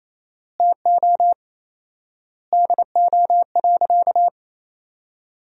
Morse code, heard as a keyed tone, about 28 wpm, 700 Hz. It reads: TO DO.
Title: 4x Word-Spacing - Sentences from Top 100 Words - 28wpm